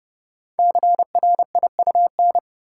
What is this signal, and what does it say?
Morse code, heard as a keyed tone, about 29 wpm, 690 Hz. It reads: CRIUN
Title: Code Groups - 5 Letters - 30wpm